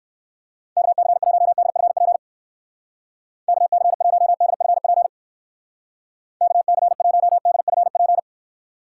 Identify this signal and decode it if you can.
Morse code, heard as a keyed tone, about 57 wpm, 700 Hz. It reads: KC1DFP KC1DFP KC1DFP